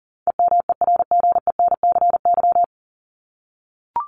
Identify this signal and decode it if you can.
Morse code, heard as a keyed tone, about 40 wpm, 700 Hz. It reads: EMERGENCY
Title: Top 1501-1750 Words 40wpm